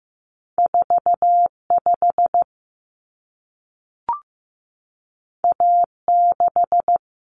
Morse code, heard as a keyed tone, about 15 wpm, 700 Hz. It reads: 45 A6
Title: Common Cut Numbers - 2-Digit 15wpm